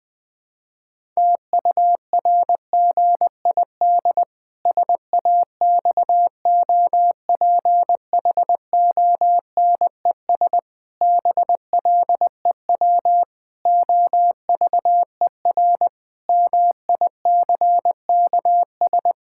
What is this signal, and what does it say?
Morse code, heard as a keyed tone, about 20 wpm, 700 Hz. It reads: TURGID SAXOPHONES BLEW OVER MICKS
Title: Pangram Challenge v2 - 20wpm